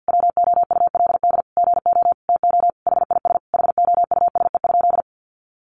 Morse code, heard as a keyed tone, about 50 wpm, 700 Hz. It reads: WOULD GO TO HIS HOUSE?
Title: Sentences from Top 600 Words 50wpm